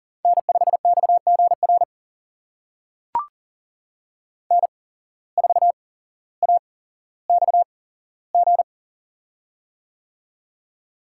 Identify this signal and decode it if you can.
Morse code, heard as a keyed tone, about 40 wpm, spaced as for 17 wpm, 700 Hz. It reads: N5XGR N4AXG
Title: USA Call Signs - 5 Characters - 8x Character Spacing - 40wpm